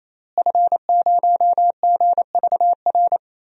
Morse code, heard as a keyed tone, about 28 wpm, 700 Hz. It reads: F0GVR
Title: DXCC-100 Call Signs - 5 Characters - Repeated 3x - 28wpm